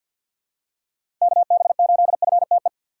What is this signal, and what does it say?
Morse code, heard as a keyed tone, about 52 wpm, 700 Hz. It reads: KB8FTE
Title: USA Call Signs - 3x Character Spacing - 50wpm